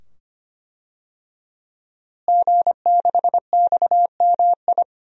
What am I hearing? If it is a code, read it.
Morse code, heard as a keyed tone, about 25 wpm, 700 Hz. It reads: G6XMI
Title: DXCC-100 Call Signs - 5 Characters - Repeated 2x - 25wpm